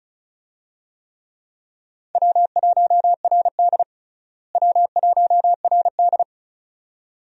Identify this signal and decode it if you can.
Morse code, heard as a keyed tone, about 35 wpm, 700 Hz. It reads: W1RD W1RD
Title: USA Call Signs - 4 Characters - Repeated 2x - 35wpm